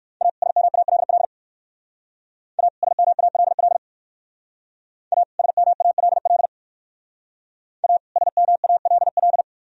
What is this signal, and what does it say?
Morse code, heard as a keyed tone, about 58 wpm, 690 Hz. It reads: A SMALL A SMALL A SMALL A SMALL